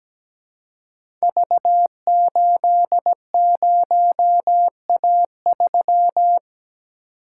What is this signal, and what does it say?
Morse code, heard as a keyed tone, about 17 wpm, 700 Hz. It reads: V80A3
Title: Code Groups - 5 Letter:Numbers - 17wpm